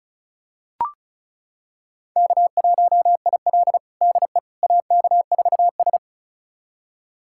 Morse code, heard as a keyed tone, about 36 wpm, 700 Hz. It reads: K1IL DE AK4S
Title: USA Callsign DE Callsign - 4 Characters - 35wpm